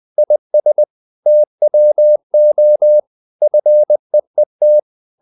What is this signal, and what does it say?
Morse code, heard as a keyed tone, about 20 wpm, 600 Hz. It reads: IS TWO FEET
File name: the old man and the sea / the old man and the sea part 1 20wpm